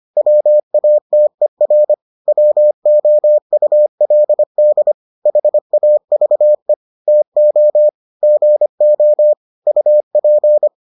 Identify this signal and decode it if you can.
Morse code, heard as a keyed tone, about 25 wpm, 600 Hz. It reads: WATER WOULD HAVE TO GO UP